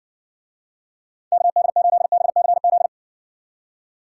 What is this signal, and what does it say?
Morse code, heard as a keyed tone, about 60 wpm, 700 Hz. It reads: KD8BCZ